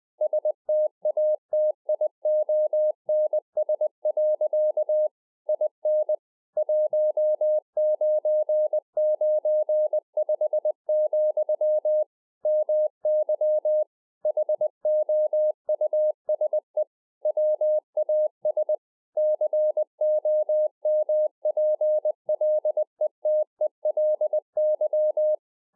Morse code, heard as a keyed tone, about 20 wpm, 610 Hz. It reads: STATIONS. IN 1995, MY HOUSE WAS COMPLETELY